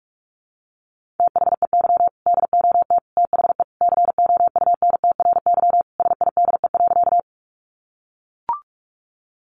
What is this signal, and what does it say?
Morse code, heard as a keyed tone, about 44 wpm, 700 Hz. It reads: THEY DOT THE COUNTRY SIDE.
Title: Sentences from Top 1500 Words 45wpm